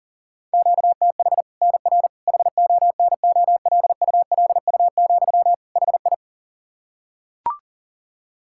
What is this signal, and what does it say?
Morse code, heard as a keyed tone, about 37 wpm, 700 Hz. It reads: QTH NR HONOLULU, HI
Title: Practice QSO - Exchange 1 - Beginning - 40wpm